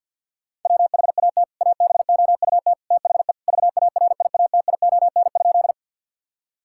Morse code, heard as a keyed tone, about 50 wpm, 700 Hz. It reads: WHAT ABOUT THE VARIATION?
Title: Sentences from Top 1600 Words 50wpm